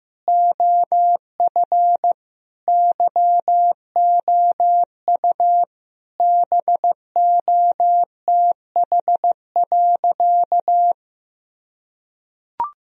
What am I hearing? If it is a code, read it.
Morse code, heard as a keyed tone, about 16 wpm, 700 Hz. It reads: OF YOU BOTH.
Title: Sentences from Top 1000 Words - Review Part 1 - 15wpm